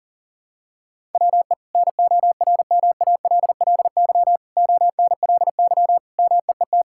Morse code, heard as a keyed tone, about 40 wpm, 700 Hz. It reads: WE NORMALLY ONLY MEET
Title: Sentences from Top 1000 Words - Review Part 6 - 40wpm